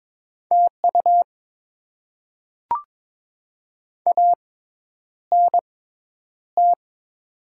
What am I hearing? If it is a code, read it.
Morse code, heard as a keyed tone, about 22 wpm, spaced as for 10 wpm, 700 Hz. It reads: TU ANT
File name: Top 20 QSO Elements - 6x Character Spacing - 22wpm